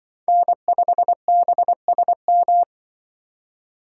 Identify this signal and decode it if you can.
Morse code, heard as a keyed tone, about 24 wpm, 700 Hz. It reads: N5BSM
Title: Speed Racing - USA 1x3 Call Signs 20wpm